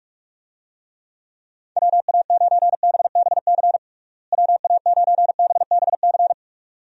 Morse code, heard as a keyed tone, about 46 wpm, 700 Hz. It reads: WA9BBC WA9BBC
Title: USA Call Signs - 6 Characters - Repeated 2x - 45wpm